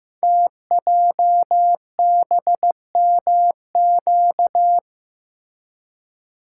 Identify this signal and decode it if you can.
Morse code, heard as a keyed tone, about 15 wpm, 700 Hz. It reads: TJBMQ
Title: Code Groups - 5 Letters - 15wpm